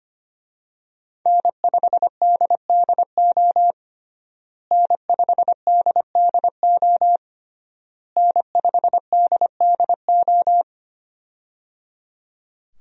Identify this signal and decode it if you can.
Morse code, heard as a keyed tone, about 25 wpm, 700 Hz. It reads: N5DDO N5DDO N5DDO
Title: USA Call Signs - 5 Characters - Repeated 3x - 25wpm